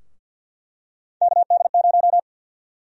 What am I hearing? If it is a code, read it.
Morse code, heard as a keyed tone, about 46 wpm, 700 Hz. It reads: KD0
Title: USA Call Sign Prefixes - 50wpm